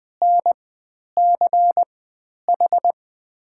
Morse code, heard as a keyed tone, about 20 wpm, spaced as for 10 wpm, 700 Hz. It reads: NCH
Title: 109 Lesson 18g - Letter m - Call Signs 20wpm